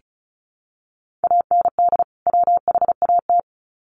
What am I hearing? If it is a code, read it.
Morse code, heard as a keyed tone, about 34 wpm, 700 Hz. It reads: AND WHAT
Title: Sets of 2 Words 35wpm